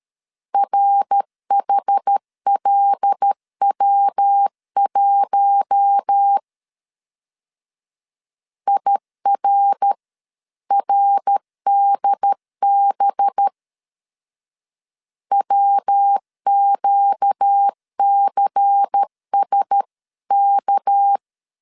Morse code, heard as a keyed tone, about 13 wpm, 800 Hz. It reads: RHLW1 IRRDB WQCSK
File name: CW / DET / 1 6 1500C S B 12wpm 3431